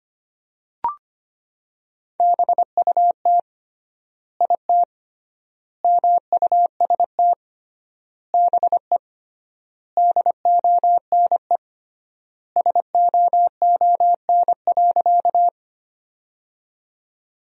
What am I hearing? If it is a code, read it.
Morse code, heard as a keyed tone, about 25 wpm, 700 Hz. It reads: BUT IT MUST BE DONE SOON.
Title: 3x Word-Spacing - Sentences from Top 500 Words - 25wpm